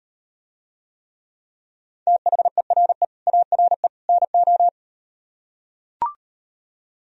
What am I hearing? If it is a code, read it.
Morse code, heard as a keyed tone, about 38 wpm, 700 Hz. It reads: THERE ARE NO